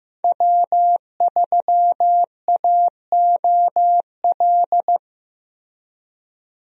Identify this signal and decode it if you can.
Morse code, heard as a keyed tone, about 15 wpm, 700 Hz. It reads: W3AOL
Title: USA Call Signs - 5 Characters - Repeated 2x - 15wpm